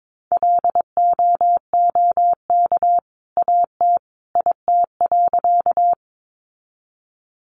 Morse code, heard as a keyed tone, about 22 wpm, 700 Hz. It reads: LOOK AT IT.
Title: Sentences from Top 100 Words 22wpm